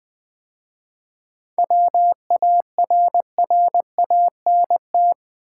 Morse code, heard as a keyed tone, about 20 wpm, 690 Hz. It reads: WARRANT
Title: 7 Letter Words - First 3 Letters Given - 20wpm